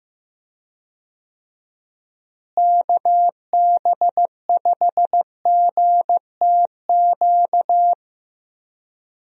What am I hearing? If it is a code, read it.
Morse code, heard as a keyed tone, about 16 wpm, 700 Hz. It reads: KB5GTQ
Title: USA Call Signs - 6 Characters - Repeated 2x - 15wpm